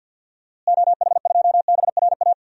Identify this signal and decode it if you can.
Morse code, heard as a keyed tone, about 52 wpm, 700 Hz. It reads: OH2BRA
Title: DXCC-100 Call Signs - 6 Characters - Repeated 2x - 50wpm